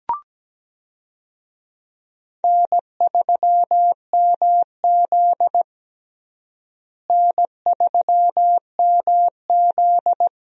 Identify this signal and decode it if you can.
Morse code, heard as a keyed tone, about 17 wpm, 700 Hz. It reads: N3MZ N3MZ
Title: USA Call Signs - 4 Characters - Repeated 2x - 17wpm